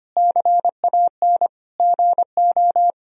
Morse code, heard as a keyed tone, about 25 wpm, 700 Hz. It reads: CAN GO
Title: Sentences from Top 1000 Words - Review Part 1 - 25wpm